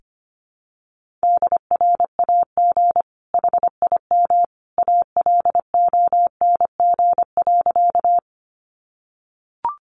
Morse code, heard as a keyed tone, about 25 wpm, 700 Hz. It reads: DRAG HIM ALONG.
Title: Sentences from Top 1500 Words 25wpm